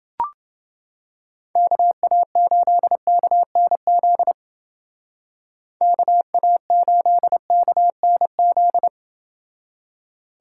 Morse code, heard as a keyed tone, about 28 wpm, 700 Hz. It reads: KA8KNZ KA8KNZ